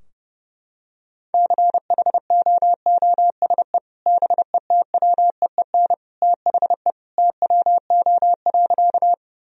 Morse code, heard as a keyed tone, about 30 wpm, 700 Hz. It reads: CHOOSE BETWEEN THE TWO.
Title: Sentences from Top 1000 Words - Review Part 4 - 30wpm